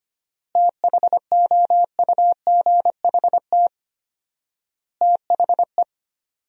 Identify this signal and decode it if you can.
Morse code, heard as a keyed tone, about 25 wpm, 700 Hz. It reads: THOUGHT THE
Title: 4x Word-Spacing - Sentences from Top 300 Words - 25wpm